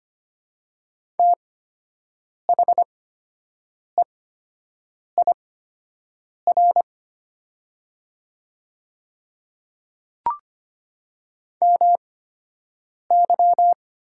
Morse code, heard as a keyed tone, about 25 wpm, spaced as for 7 wpm, 700 Hz. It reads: THEIR MY
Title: 8x Character Spacing - Top 100 Words - 25wpm